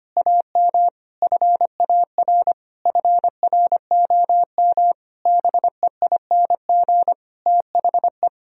Morse code, heard as a keyed tone, about 25 wpm, 700 Hz. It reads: AM FAR FROM BEING THE
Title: Sentences from Top 1000 Words - Review Part 3 - 25wpm